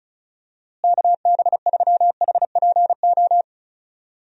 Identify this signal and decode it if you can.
Morse code, heard as a keyed tone, about 36 wpm, 700 Hz. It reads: KB3HPO
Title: USA Call Signs - 6 Characters - Repeated 3x - 35wpm